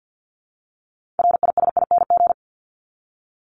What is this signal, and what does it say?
Morse code, heard as a keyed tone, about 50 wpm, 700 Hz. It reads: RISING
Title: ING Suffix - Top 100 Words with Base Words 50wpm